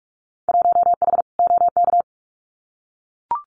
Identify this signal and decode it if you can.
Morse code, heard as a keyed tone, about 44 wpm, 700 Hz. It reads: 1H OK
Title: Winter Field Day Exchanges 45wpm